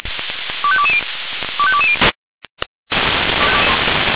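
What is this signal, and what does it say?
Five-tone selective call (ZVEI), heard as quick runs of five tones, tones of about 70 ms. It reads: 25220 25220
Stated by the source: ZVEI five-tone call 25212 received on SDR.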